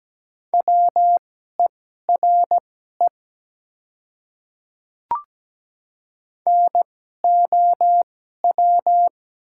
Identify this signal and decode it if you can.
Morse code, heard as a keyed tone, about 17 wpm, spaced as for 12 wpm, 700 Hz. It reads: WERE NOW
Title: 2x Character Spacing - Top 100 Words - 17wpm